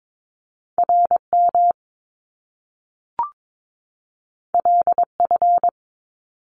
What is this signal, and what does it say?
Morse code, heard as a keyed tone, about 22 wpm, 700 Hz. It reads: RM LF
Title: Top Two Letter Combinations in Words 22wpm